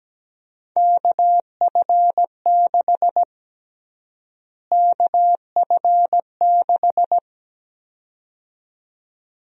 Morse code, heard as a keyed tone, about 17 wpm, 700 Hz. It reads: KF6 KF6
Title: USA Call Sign Prefixes - Repeated 2x - 17wpm